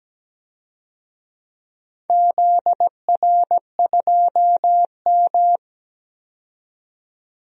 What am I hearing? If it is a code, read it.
Morse code, heard as a keyed tone, about 17 wpm, 700 Hz. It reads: ZR2M
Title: DXCC-100 Call Signs - 4 Characters - 17wpm